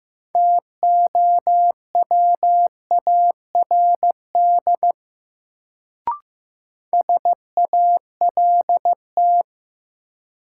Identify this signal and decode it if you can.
Morse code, heard as a keyed tone, about 15 wpm, 700 Hz. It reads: TOWARD SALT
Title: Sets of 5 Words - New Word Drill - 15wpm